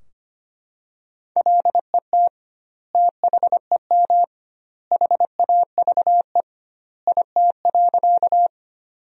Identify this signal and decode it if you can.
Morse code, heard as a keyed tone, about 25 wpm, 700 Hz. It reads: LET THEM HAVE IT.
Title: 2x Word-Spacing - Sentences from Top 300 Words - 25wpm